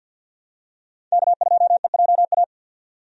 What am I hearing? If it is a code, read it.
Morse code, heard as a keyed tone, about 45 wpm, 700 Hz. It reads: K2EJA
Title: USA Call Signs - 5 Characters - 50wpm